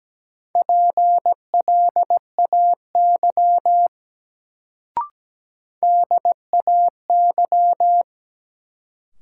Morse code, heard as a keyed tone, about 17 wpm, 700 Hz. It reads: PLAY DAY
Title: Sentences from Top 200 Words - New Word Drill - 17wpm